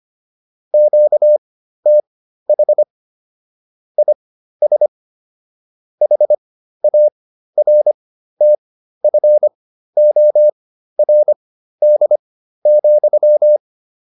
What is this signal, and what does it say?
Morse code, heard as a keyed tone, about 25 wpm, spaced as for 13 wpm, 600 Hz. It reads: QTH IS HARTFORD,
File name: QSO401 13